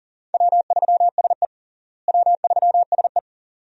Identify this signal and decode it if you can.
Morse code, heard as a keyed tone, about 40 wpm, 700 Hz. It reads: W3SE W3SE
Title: USA Call Signs - 4 Characters - Repeated 2x - 40wpm